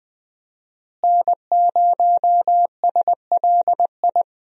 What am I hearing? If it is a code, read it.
Morse code, heard as a keyed tone, about 20 wpm, 700 Hz. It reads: N0SLI
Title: USA Call Signs - 5 Characters - Repeated 2x - 20wpm